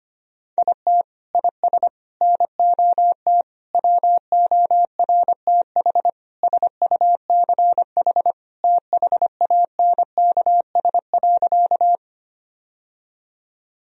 Morse code, heard as a keyed tone, about 25 wpm, 700 Hz. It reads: IT IS NOT WORTH SUCH THANKS.
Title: Sentences from Top 1000 Words - Review Part 3 - 25wpm